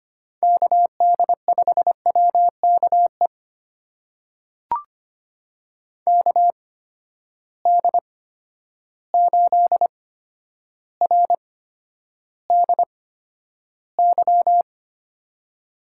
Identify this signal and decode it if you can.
Morse code, heard as a keyed tone, about 25 wpm, spaced as for 11 wpm, 700 Hz. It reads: KD5WKE KD8RDY
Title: USA Call Signs - 6 Characters - 8x Character Spacing - 25wpm